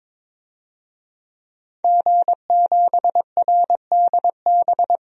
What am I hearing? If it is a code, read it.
Morse code, heard as a keyed tone, about 22 wpm, 700 Hz. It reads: G7RDB
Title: DXCC-100 Call Signs - 5 Characters - 22wpm